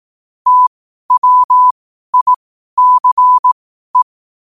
Morse code, heard as a keyed tone, about 18 wpm, spaced as for 12 wpm, 1000 Hz. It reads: TWICE
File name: qotdfiles / 12wpm